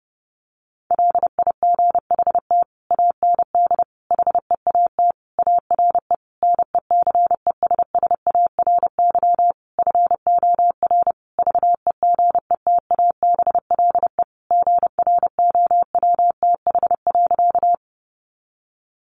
Morse code, heard as a keyed tone, about 30 wpm, 700 Hz. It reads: LIGHT AND HEAT ARE NECESSARY FOR VEGETABLE GROWTH.
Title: Sentences from Top 1300 Words 30wpm